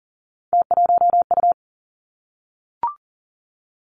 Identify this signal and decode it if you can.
Morse code, heard as a keyed tone, about 42 wpm, 700 Hz. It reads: T1U
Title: Three Challenging Letter-Number Combinations 40wpm